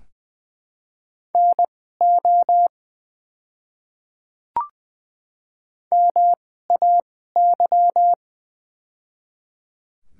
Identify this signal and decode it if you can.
Morse code, heard as a keyed tone, about 20 wpm, spaced as for 15 wpm, 700 Hz. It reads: NO MAY
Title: 2x Character Spacing - Top 100 Words - 20wpm